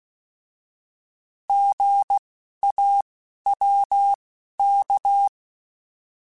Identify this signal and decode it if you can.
Morse code, heard as a keyed tone, about 16 wpm, spaced as for 11 wpm, 800 Hz. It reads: GAWK
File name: Alphabetical / Alphabetical mp3s / Gs @10